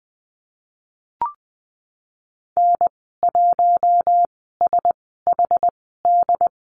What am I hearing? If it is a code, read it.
Morse code, heard as a keyed tone, about 20 wpm, spaced as for 14 wpm, 700 Hz. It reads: N1SHD
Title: 069 Lesson 12g - Letter d - Call Signs 20wpm